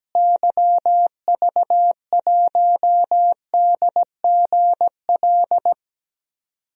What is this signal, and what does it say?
Morse code, heard as a keyed tone, about 17 wpm, 700 Hz. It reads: YV1DGL
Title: DXCC-100 Call Signs - 6 Characters - Repeated 3x - 17wpm